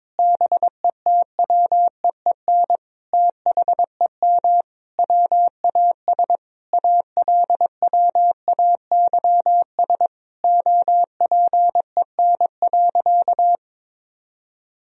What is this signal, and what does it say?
Morse code, heard as a keyed tone, about 22 wpm, 700 Hz. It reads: BETWEEN THEM WAS ALWAYS OPEN.
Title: Sentences from Top 1000 Words - Review Part 1 - 22wpm